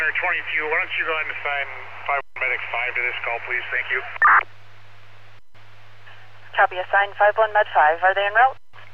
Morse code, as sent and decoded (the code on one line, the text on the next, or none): none